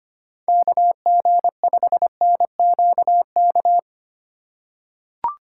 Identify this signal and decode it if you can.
Morse code, heard as a keyed tone, about 25 wpm, 700 Hz. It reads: KG5NQK
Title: USA Call Signs - 6 Characters - 2x Character Spacing - 25wpm